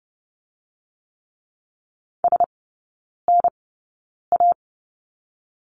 Morse code, heard as a keyed tone, about 30 wpm, 700 Hz. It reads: S N A
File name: Lesson 19e - Letter w - Three Characters 30wpm